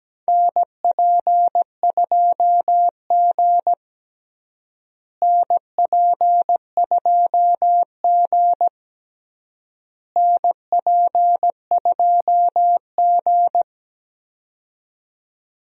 Morse code, heard as a keyed tone, about 17 wpm, 700 Hz. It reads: NP2G NP2G NP2G
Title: USA Call Signs - 4 Characters - Repeated 3x - 17wpm